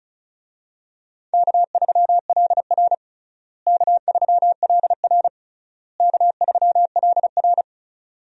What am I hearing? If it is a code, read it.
Morse code, heard as a keyed tone, about 36 wpm, 700 Hz. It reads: K3LR K3LR K3LR
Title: USA Call Signs - 4 Characters - Repeated 3x - 35wpm